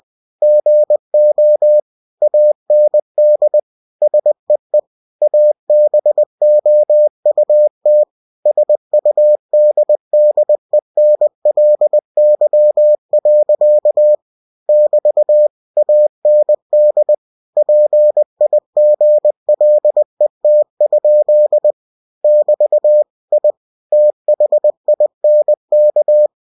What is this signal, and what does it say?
Morse code, heard as a keyed tone, about 20 wpm, 600 Hz. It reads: GO AND SEE ABOUT SUDDENLY. <BT> AND PIGLET? <BT> I THINK
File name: cw-books / winnie-the-pooh / winnie-20wpm-page-0070